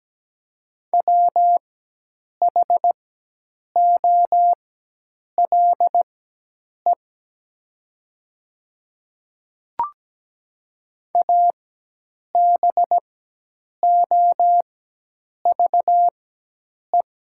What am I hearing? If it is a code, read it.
Morse code, heard as a keyed tone, about 17 wpm, spaced as for 8 wpm, 700 Hz. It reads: WHOLE ABOVE
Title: 4x Character Spacing - Top 500 Words - 17wpm